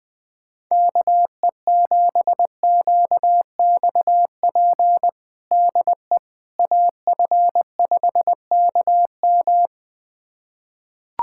Morse code, heard as a keyed tone, about 20 wpm, 700 Hz. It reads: KE7QXP DE AF5KM
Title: USA Callsign DE Callsign 20wpm